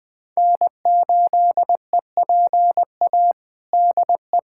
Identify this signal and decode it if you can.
Morse code, heard as a keyed tone, about 21 wpm, 700 Hz. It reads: N8EPA DE
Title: USA Callsign DE Callsign - 5 Characters - 20wpm